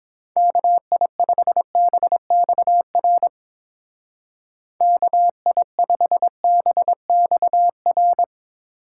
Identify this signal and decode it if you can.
Morse code, heard as a keyed tone, about 25 wpm, 700 Hz. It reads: KI5BXR KI5BXR
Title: Speed Racing - USA 2x3 Call Signs 22wpm